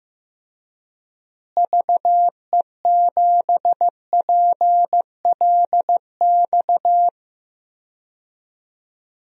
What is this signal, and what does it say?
Morse code, heard as a keyed tone, about 15 wpm, 700 Hz. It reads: VE7PLX
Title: DXCC-100 Call Signs - 6 Characters - Repeated 2x - 15wpm